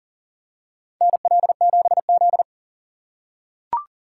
Morse code, heard as a keyed tone, about 40 wpm, 700 Hz. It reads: NL7Z
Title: USA Call Signs - 4 Characters - 6x Character Spacing - 40wpm